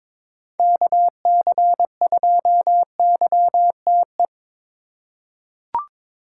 Morse code, heard as a keyed tone, about 21 wpm, 700 Hz. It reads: KC2YTE
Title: USA Call Signs - 6 Characters - 8x Character Spacing - 22wpm